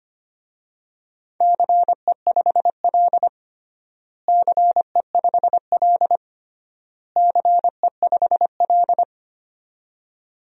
Morse code, heard as a keyed tone, about 25 wpm, 700 Hz. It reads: CE5L CE5L CE5L
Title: DXCC-100 Call Signs - 4 Characters - Repeated 3x - 25wpm